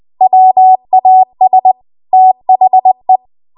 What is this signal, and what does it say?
Morse code, heard as a keyed tone, about 20 wpm, 750 Hz. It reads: WAS THE
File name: Archive / 20 WPM / 20WPM